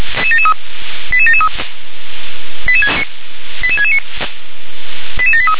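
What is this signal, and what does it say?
Five-tone selective call (ZVEI), heard as quick runs of five tones, tones of about 70 ms. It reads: 88773 88669 88783
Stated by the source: ZVEI five-tone selective call, code 88747, received by SDR.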